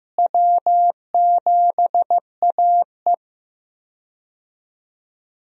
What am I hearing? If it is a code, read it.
Morse code, heard as a keyed tone, about 15 wpm, 700 Hz. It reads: W7AE
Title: USA Call Signs - 4 Characters - Repeated 2x - 15wpm